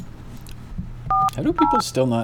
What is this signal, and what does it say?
DTMF, heard as telephone touch tones, tones of about 200 ms, with speech. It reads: 408